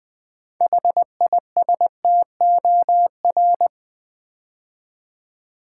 Morse code, heard as a keyed tone, about 20 wpm, 700 Hz. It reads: HISTOR
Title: Build A Word - 3 Letters At A Time - 9-Letter Words - 20wpm